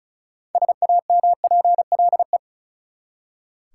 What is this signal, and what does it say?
Morse code, heard as a keyed tone, about 34 wpm, 700 Hz. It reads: SAMPLE
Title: Six Letter Words - 35wpm